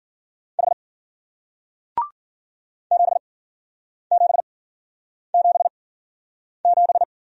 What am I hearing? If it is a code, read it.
Morse code, heard as a keyed tone, about 52 wpm, spaced as for 9 wpm, 700 Hz. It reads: H 7777